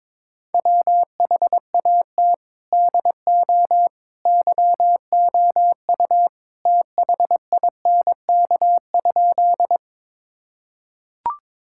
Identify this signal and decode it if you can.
Morse code, heard as a keyed tone, about 22 wpm, 700 Hz. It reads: WHAT DO YOU THINK?
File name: practice / 22wpm / Sentences with 2-Word Phrases - 22wpm